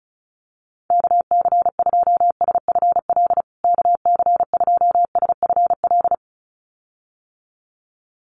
Morse code, heard as a keyed tone, about 34 wpm, 700 Hz. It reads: KC2SFL KC2SFL
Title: USA Call Signs Repeated 2x 35wpm